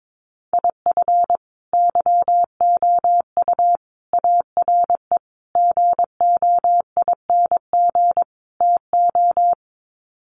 Morse code, heard as a keyed tone, about 22 wpm, 700 Hz. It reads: IF YOU ARE GOING TO
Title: Sets of 5 Words Encore2 22wpm